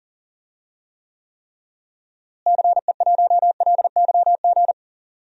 Morse code, heard as a keyed tone, about 40 wpm, 700 Hz. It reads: CE1LYG
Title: DXCC-100 Call Signs - 6 Characters - Repeated 3x - 40wpm